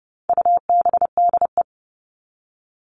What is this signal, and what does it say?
Morse code, heard as a keyed tone, about 30 wpm, 700 Hz. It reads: UBDE
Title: All Cut Numbers - 4-Digit 30wpm